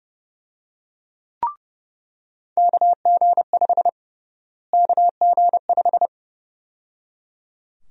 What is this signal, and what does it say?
Morse code, heard as a keyed tone, about 30 wpm, 700 Hz. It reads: KG5 KG5
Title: USA Call Sign Prefixes - Repeated 2x - 30wpm